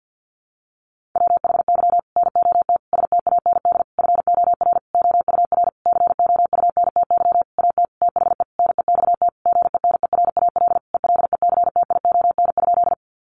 Morse code, heard as a keyed tone, about 50 wpm, 700 Hz. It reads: WHY NOT STAND FOR OUR COUNTY AT THE NEXT GENERAL ELECTION?
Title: Sentences from Top 1500 Words 50wpm